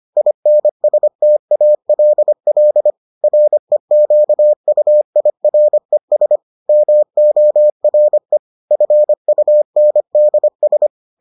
Morse code, heard as a keyed tone, about 25 wpm, 600 Hz. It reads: INSTALL REQUIRES MORE FUNDS